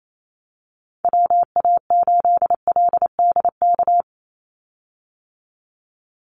Morse code, heard as a keyed tone, about 28 wpm, 700 Hz. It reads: WA8LDK
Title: USA Call Signs 28wpm